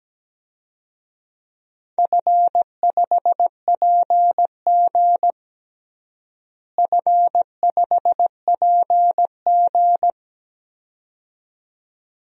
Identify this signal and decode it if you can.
Morse code, heard as a keyed tone, about 17 wpm, 700 Hz. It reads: F5PG F5PG
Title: DXCC-100 Call Signs - 4 Characters - Repeated 2x - 17wpm